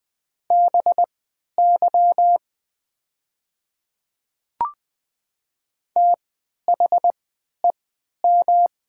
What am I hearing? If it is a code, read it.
Morse code, heard as a keyed tone, about 20 wpm, spaced as for 11 wpm, 700 Hz. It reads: BY THEM
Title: 3x Character Spacing - Top 100 Words - 20wpm